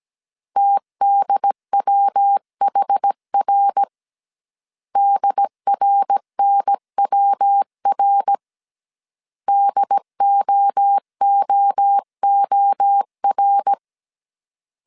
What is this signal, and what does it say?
Morse code, heard as a keyed tone, about 17 wpm, 800 Hz. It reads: TDWHR DRNWR DOOOR
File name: CW / DET / 1 3 1500C S C 16wpm 2148